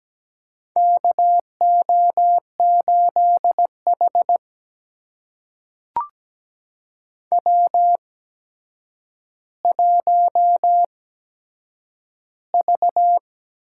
Morse code, heard as a keyed tone, about 17 wpm, spaced as for 8 wpm, 700 Hz. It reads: KO8H W1V
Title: USA Call Signs - 4 Characters - 8x Character Spacing - 17wpm